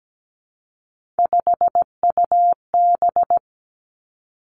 Morse code, heard as a keyed tone, about 17 wpm, 700 Hz. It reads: 5UB
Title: Three Challenging Letter-Number Combinations 17wpm